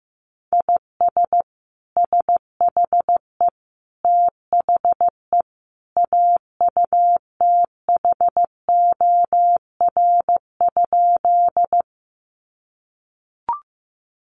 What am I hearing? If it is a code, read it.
Morse code, heard as a keyed tone, about 15 wpm, 700 Hz. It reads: IS SHE THE AUTHOR?
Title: Sentences from Top 1200 Words 15wpm